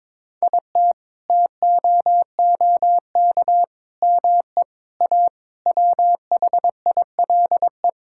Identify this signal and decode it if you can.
Morse code, heard as a keyed tone, about 22 wpm, 700 Hz. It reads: IT TOOK ME A WHILE
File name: practice / 22wpm / Sets of 8 Words - 22wpm